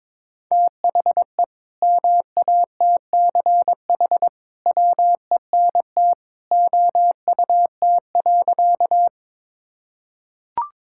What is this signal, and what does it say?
Morse code, heard as a keyed tone, about 21 wpm, 700 Hz. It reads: THE MATCH WENT OUT.
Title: Sentences from Top 1600 Words 22wpm